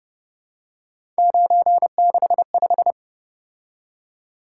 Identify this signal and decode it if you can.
Morse code, heard as a keyed tone, about 30 wpm, 700 Hz. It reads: 965